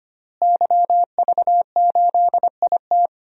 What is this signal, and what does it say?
Morse code, heard as a keyed tone, about 25 wpm, 700 Hz. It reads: YV8IT
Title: DXCC-100 Call Signs - 5 Characters - Repeated 3x - 25wpm